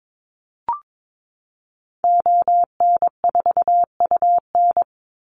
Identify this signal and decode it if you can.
Morse code, heard as a keyed tone, about 22 wpm, 700 Hz. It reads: ON4UN
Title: Call Signs Repeated 2x - 22wpm